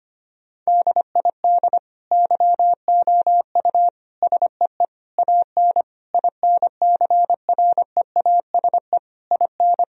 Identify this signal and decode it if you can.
Morse code, heard as a keyed tone, about 25 wpm, 700 Hz. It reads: DID YOU SEE AN INCREASE IN
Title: Sentences from Top 1000 Words - Review Part 4 - 25wpm